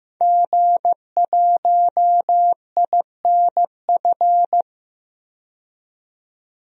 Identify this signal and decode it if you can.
Morse code, heard as a keyed tone, about 15 wpm, 700 Hz. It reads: G1INF
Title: DXCC-100 Call Signs - 5 Characters - 15wpm